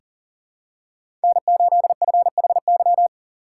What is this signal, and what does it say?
Morse code, heard as a keyed tone, about 40 wpm, 700 Hz. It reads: N8FHY
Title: USA Call Signs - 5 Characters - 6x Character Spacing - 40wpm